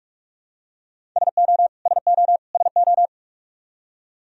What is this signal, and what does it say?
Morse code, heard as a keyed tone, about 46 wpm, 700 Hz. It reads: SO SO SO
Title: Top 100 Words Repeated 45wpm